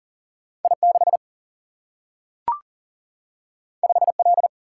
Morse code, heard as a keyed tone, about 40 wpm, 700 Hz. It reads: I 6 5 L
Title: Single-Letter Single-Number 40wpm